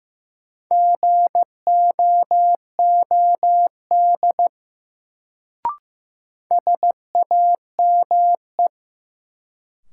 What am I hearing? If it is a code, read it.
Morse code, heard as a keyed tone, about 15 wpm, 700 Hz. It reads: GOOD SAME
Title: Sentences from Top 200 Words - New Word Drill - 15wpm